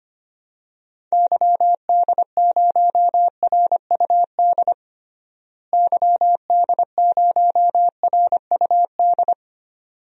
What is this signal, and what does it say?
Morse code, heard as a keyed tone, about 25 wpm, 700 Hz. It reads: YD0RUD YD0RUD
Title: DXCC-100 Call Signs - 6 Characters - Repeated 2x - 25wpm